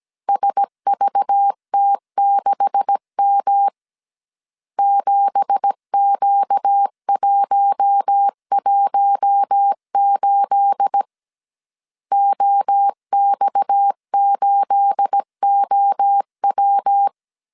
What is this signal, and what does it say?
Morse code, heard as a keyed tone, about 17 wpm, 800 Hz. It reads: SVT6M 7Q118 OX8OW
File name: CW / DET / 1 9 1500C S C 16wpm 2909